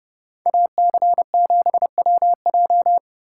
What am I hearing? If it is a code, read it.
Morse code, heard as a keyed tone, about 29 wpm, 700 Hz. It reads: AC7WJ